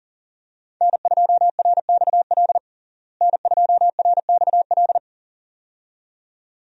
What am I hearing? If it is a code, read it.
Morse code, heard as a keyed tone, about 37 wpm, 700 Hz. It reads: N2RXL N2RXL
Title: USA Call Signs - 5 Characters - Repeated 2x - 40wpm